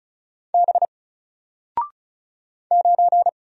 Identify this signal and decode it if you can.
Morse code, heard as a keyed tone, about 36 wpm, 700 Hz. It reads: B 9
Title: Single Letter-Number Alt 35wpm